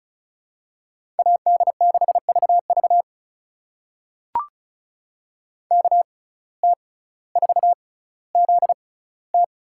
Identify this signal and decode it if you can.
Morse code, heard as a keyed tone, about 36 wpm, spaced as for 18 wpm, 700 Hz. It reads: AD6VV KT4ZT
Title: USA Call Signs - 5 Characters - 6x Character Spacing - 35wpm